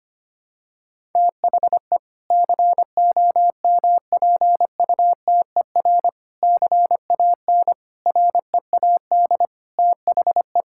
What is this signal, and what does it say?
Morse code, heard as a keyed tone, about 25 wpm, 700 Hz. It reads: THE COMPUTER CAN READ THE
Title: Sentences from Top 1000 Words - Review Part 6 - 25wpm